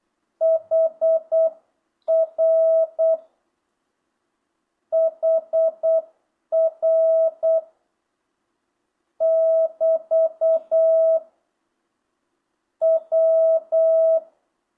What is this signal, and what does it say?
Morse code, heard as a keyed tone, about 8 wpm, 650 Hz. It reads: HR HR <BT> W
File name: jh8cbh / 2025cw